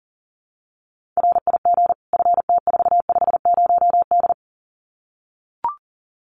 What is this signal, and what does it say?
Morse code, heard as a keyed tone, about 40 wpm, 700 Hz. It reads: RIG FT450D
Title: Common QSO Phrases 40wpm